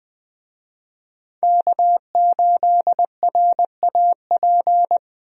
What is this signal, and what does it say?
Morse code, heard as a keyed tone, about 21 wpm, 700 Hz. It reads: K8RAP
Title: USA Call Signs - 5 Characters - Repeated 3x - 20wpm